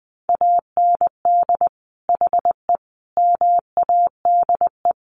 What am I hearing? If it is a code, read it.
Morse code, heard as a keyed tone, about 20 wpm, 700 Hz. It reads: AND HE MADE
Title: Sentences from Top 1500 Words 20wpm